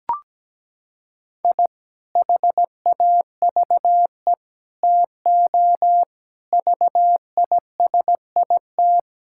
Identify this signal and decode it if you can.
Morse code, heard as a keyed tone, about 17 wpm, 700 Hz. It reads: I HAVE TO VISIT
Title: Sentences from Top 1000 Words - Review Part 6 - 17wpm